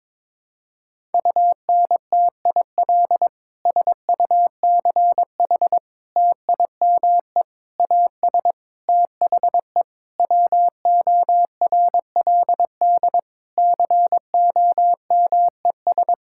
Morse code, heard as a keyed tone, about 21 wpm, 700 Hz. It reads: UNTIL SUCH TIME AS THE WORLD COMES
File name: Sets of 9 Words - 22wpm